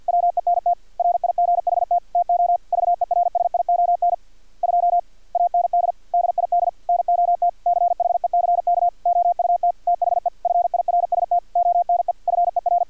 Morse code, heard as a keyed tone, about 50 wpm, 700 Hz. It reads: WENT RIGHT TO VERSION 2 AND DID NOT CHECK OUT THE FIRST ONE FER
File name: archived / CRSnet